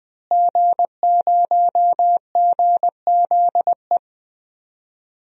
Morse code, heard as a keyed tone, about 21 wpm, 700 Hz. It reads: G0GZE
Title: DXCC-100 Call Signs - 5 Characters - Repeated 3x - 20wpm